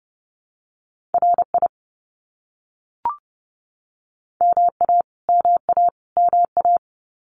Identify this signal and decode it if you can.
Morse code, heard as a keyed tone, about 30 wpm, 700 Hz. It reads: RI MA MA MA
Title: US States Repeated 3x 30wpm